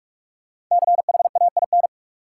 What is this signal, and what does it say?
Morse code, heard as a keyed tone, about 43 wpm, 700 Hz. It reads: CHAIN
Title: Sentences from Top 1500 Words - New Word Drill Alt - 45wpm